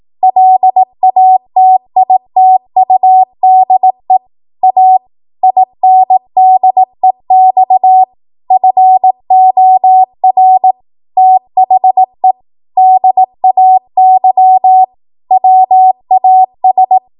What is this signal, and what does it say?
Morse code, heard as a keyed tone, about 18 wpm, 750 Hz. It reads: LATITUDE A INDEX FOR THE DAY WAS
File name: Archive / 18 WPM / 18WPM